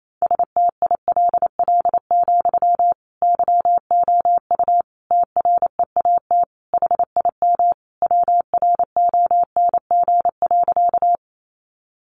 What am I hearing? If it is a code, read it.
Morse code, heard as a keyed tone, about 28 wpm, 700 Hz. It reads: STILL, YOU TREAT HIM WRONG.
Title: Sentences from Top 1300 Words 28wpm